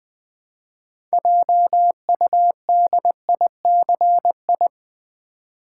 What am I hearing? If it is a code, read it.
Morse code, heard as a keyed tone, about 20 wpm, 700 Hz. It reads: JUDICI
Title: Build A Word - 2 Letters At A Time - 8-Letter Words - 20wpm